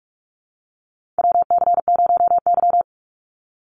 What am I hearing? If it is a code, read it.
Morse code, heard as a keyed tone, about 46 wpm, 690 Hz. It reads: WC0Y
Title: NA SOTA Chaser Callsigns 45wpm